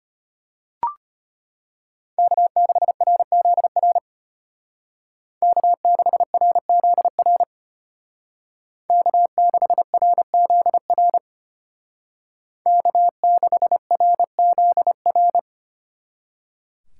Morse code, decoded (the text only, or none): K6RZR K6RZR K6RZR K6RZR